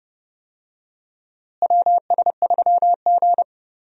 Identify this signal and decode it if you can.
Morse code, heard as a keyed tone, about 30 wpm, 700 Hz. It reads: WS3G
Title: USA Call Signs - 4 Characters - Repeated 3x - 30wpm